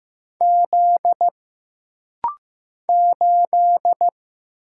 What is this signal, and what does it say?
Morse code, heard as a keyed tone, about 15 wpm, 700 Hz. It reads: Z 8
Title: Single Letter-Number Alt - Rapid-Fire 15wpm